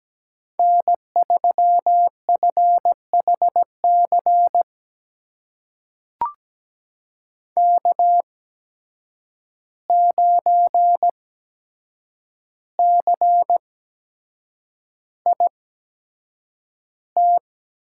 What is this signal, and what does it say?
Morse code, heard as a keyed tone, about 17 wpm, spaced as for 7 wpm, 700 Hz. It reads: N3FHC K9CIT
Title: USA Call Signs - 5 Characters - 8x Character Spacing - 17wpm